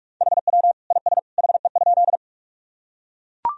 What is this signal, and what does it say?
Morse code, heard as a keyed tone, about 44 wpm, 700 Hz. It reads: HW IS HE?
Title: Sentences from Top 100 Words and Abbreviations 45wpm